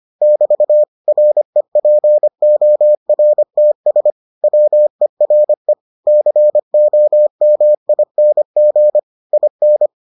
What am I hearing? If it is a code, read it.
Morse code, heard as a keyed tone, about 25 wpm, 600 Hz. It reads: <BT> REPORTS WERE COMING IN